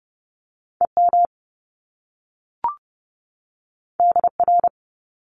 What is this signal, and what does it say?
Morse code, heard as a keyed tone, about 30 wpm, 700 Hz. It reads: EM DR
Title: Top Two Letter Combinations in Words 30wpm